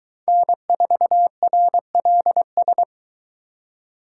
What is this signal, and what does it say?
Morse code, heard as a keyed tone, about 22 wpm, 700 Hz. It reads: N4RLS